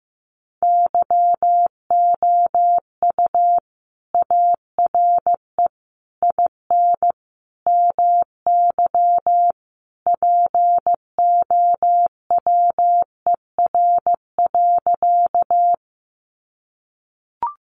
Morse code, heard as a keyed tone, about 15 wpm, 700 Hz. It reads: YOU ARE IN MY POWER.
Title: Sentences from Top 500 Words 15wpm